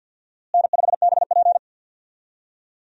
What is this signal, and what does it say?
Morse code, heard as a keyed tone, about 52 wpm, 700 Hz. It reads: N5BP
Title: USA Call Signs - 4 Characters - Repeated 3x - 50wpm